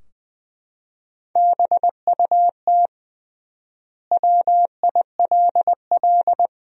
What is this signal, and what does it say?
Morse code, heard as a keyed tone, about 20 wpm, 700 Hz. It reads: BUT WILL
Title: 3x Word-Spacing - Sentences from Top 100 Words - 20wpm